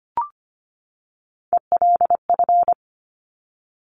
Morse code, heard as a keyed tone, about 25 wpm, 700 Hz. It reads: ELF
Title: Top Three Letter Combinations in Words 25wpm